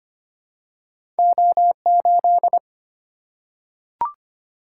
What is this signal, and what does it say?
Morse code, heard as a keyed tone, about 25 wpm, 700 Hz. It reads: O8
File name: Two Letter-Number Combinations - 25wpm